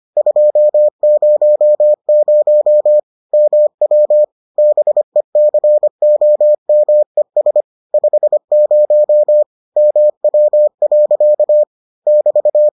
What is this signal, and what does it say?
Morse code, heard as a keyed tone, about 25 wpm, 600 Hz. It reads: 200 MW BECOMES 50 MW. <BT>